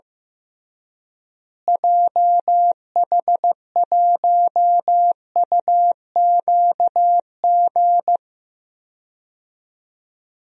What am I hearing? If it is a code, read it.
Morse code, heard as a keyed tone, about 15 wpm, 700 Hz. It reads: JH1UQG
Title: DXCC-100 Call Signs - 6 Characters - Repeated 3x - 15wpm